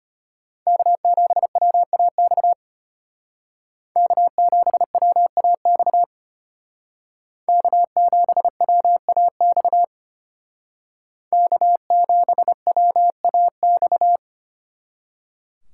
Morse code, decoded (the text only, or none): K7WAX K7WAX K7WAX K7WAX